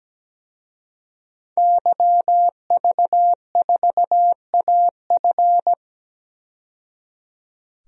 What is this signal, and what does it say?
Morse code, heard as a keyed tone, about 17 wpm, 700 Hz. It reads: YV4AF
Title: DXCC-100 Call Signs - 5 Characters - 17wpm